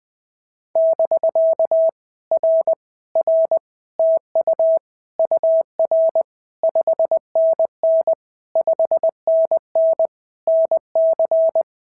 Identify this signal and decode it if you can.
Morse code, heard as a keyed tone, about 18 wpm, 650 Hz. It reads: <BK> R R TU UR 5NN 5NN NC
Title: POTA QSO Practice - 20wpm